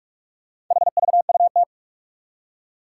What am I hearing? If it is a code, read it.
Morse code, heard as a keyed tone, about 44 wpm, 700 Hz. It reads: HVUT
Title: Four Challenging Letter-Number Combinations 45wpm